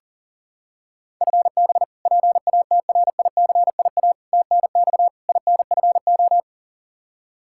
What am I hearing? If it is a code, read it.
Morse code, heard as a keyed tone, about 40 wpm, 700 Hz. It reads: FB PATRICIA TNX INFO
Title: Practice QSO - Exchange 3 - More Info - 40wpm